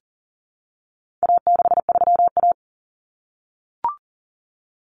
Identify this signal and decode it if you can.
Morse code, heard as a keyed tone, about 40 wpm, 700 Hz. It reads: A63A
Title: Common Cut Numbers - 4-Digit 40wpm